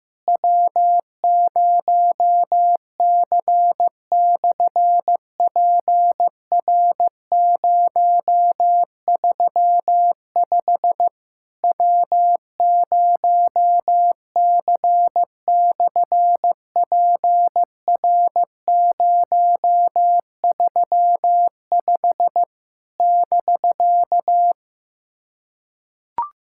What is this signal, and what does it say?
Morse code, heard as a keyed tone, about 15 wpm, 700 Hz. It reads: W0C/PR035 W0C/PR035 <BK>
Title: SOTA QSO Practice 15wpm